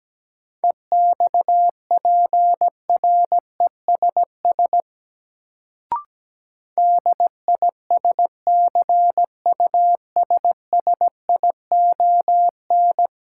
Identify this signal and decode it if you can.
Morse code, heard as a keyed tone, about 17 wpm, 700 Hz. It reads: EXPRESS DISCUSSION
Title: Sentences from Top 1100 Words - New Word Drill Alt - 17wpm